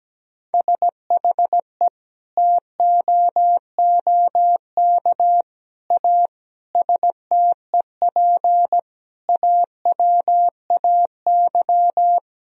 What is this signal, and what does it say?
Morse code, heard as a keyed tone, about 17 wpm, 700 Hz. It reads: SHE TOOK A STEP AWAY
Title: Sentences from Top 1000 Words - Review Part 2 - 17wpm